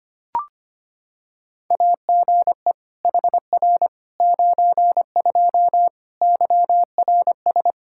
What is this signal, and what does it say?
Morse code, heard as a keyed tone, about 25 wpm, 700 Hz. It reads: AGE HR 92 YRS
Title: Practice QSO - Exchange 3 - More Info - 25wpm